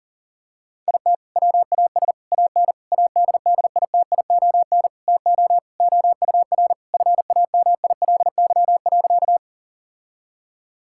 Morse code, heard as a keyed tone, about 37 wpm, 700 Hz. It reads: IT WAS AN ADDITION TO OUR FAMILY.
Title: Sentences from Top 1000 Words - Review Part 6 - 40wpm